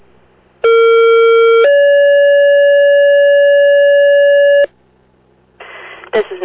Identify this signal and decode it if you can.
Two-tone page, heard as a one-second tone, then a three-second tone, about 460 then 570 Hz, with speech. It is heard from 0.6 to 4.7 s.